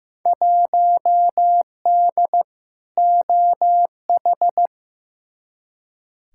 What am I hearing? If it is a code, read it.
Morse code, heard as a keyed tone, about 15 wpm, 700 Hz. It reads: 1D OH
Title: Field Day Exchanges - 15wpm